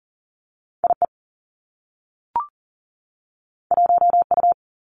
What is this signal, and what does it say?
Morse code, heard as a keyed tone, about 42 wpm, 700 Hz. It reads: IE 1U
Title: Two Challenging Letter-Number Combinations 40wpm